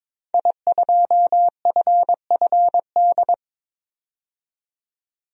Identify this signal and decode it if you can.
Morse code, heard as a keyed tone, about 22 wpm, 700 Hz. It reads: I2FFD
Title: DXCC-100 Call Signs - 5 Characters - 22wpm